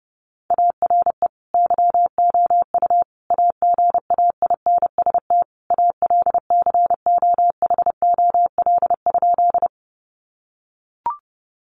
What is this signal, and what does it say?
Morse code, heard as a keyed tone, about 30 wpm, 700 Hz. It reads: ARE YOU AGAINST ALCOHOL?
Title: Sentences from Top 1400 Words 30wpm